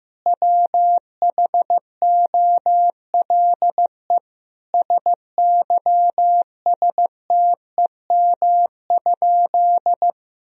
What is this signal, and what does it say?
Morse code, heard as a keyed tone, about 16 wpm, 700 Hz. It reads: WHOLE SYSTEM?
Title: Sentences from Top 1000 Words - Review Part 3 - 15wpm